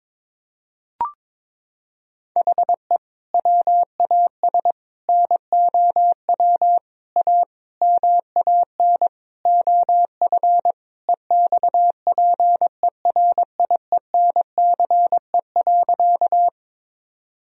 Morse code, heard as a keyed tone, about 21 wpm, 700 Hz. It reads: HE WAS NOW A MAN OF EXPERIENCE.
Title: Sentences from Top 1000 Words - Review Part 3 - 22wpm